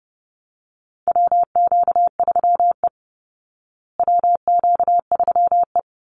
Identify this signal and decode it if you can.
Morse code, heard as a keyed tone, about 30 wpm, 700 Hz. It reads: WQ3E WQ3E
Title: Call Signs Repeated 3x - 30wpm